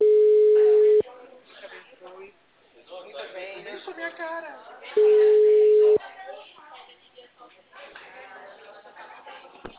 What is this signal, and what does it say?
Ringback tone (Europe, 425 Hz), with speech.
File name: out- -2030- - - .73